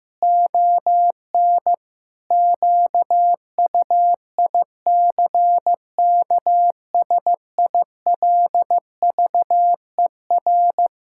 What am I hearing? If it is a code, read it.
Morse code, heard as a keyed tone, about 16 wpm, 700 Hz. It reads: ON QUICKSILVER
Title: Pangram Challenge v2 - 15wpm